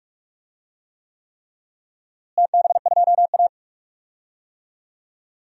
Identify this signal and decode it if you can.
Morse code, heard as a keyed tone, about 41 wpm, 690 Hz. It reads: TB2A
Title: DXCC-100 Call Signs - 4 Characters - Repeated 3x - 45wpm